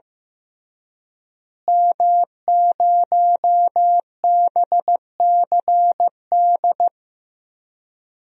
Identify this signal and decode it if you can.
Morse code, heard as a keyed tone, about 15 wpm, 700 Hz. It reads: M0BCD
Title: DXCC-100 Call Signs - 5 Characters - Repeated 3x - 15wpm